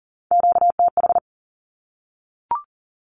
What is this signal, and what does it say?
Morse code, heard as a keyed tone, about 40 wpm, 700 Hz. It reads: QTH
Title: Q-Codes Repeated 3x 40wpm